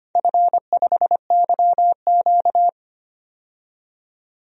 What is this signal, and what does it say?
Morse code, heard as a keyed tone, about 25 wpm, 700 Hz. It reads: F5YQ
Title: DXCC-100 Call Signs - 4 Characters - 25wpm